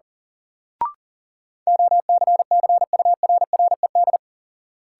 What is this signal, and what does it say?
Morse code, heard as a keyed tone, about 37 wpm, 700 Hz. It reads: OCCURRED
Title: Sets of 5 Words Encore - New Word Drill - 40wpm